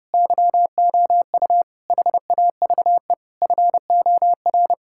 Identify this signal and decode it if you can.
Morse code, heard as a keyed tone, about 30 wpm, 700 Hz. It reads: YOU HAVE FOR
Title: Sentences from Top 1000 Words - Review Part 5 - 30wpm